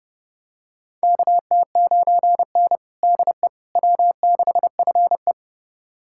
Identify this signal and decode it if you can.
Morse code, heard as a keyed tone, about 30 wpm, 700 Hz. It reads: KT9N DE W6FE
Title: USA Callsign DE Callsign - 4 Characters - 30wpm